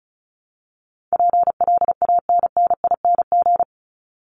Morse code, heard as a keyed tone, about 34 wpm, 700 Hz. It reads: PLANNING
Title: ING Suffix - Top 100 Words with Base Words 35wpm